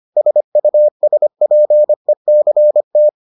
Morse code, heard as a keyed tone, about 25 wpm, 600 Hz. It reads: SUSPECT